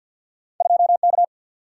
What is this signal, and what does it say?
Morse code, heard as a keyed tone, about 53 wpm, 700 Hz. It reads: 2K
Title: Single-Letter Single-Number 50wpm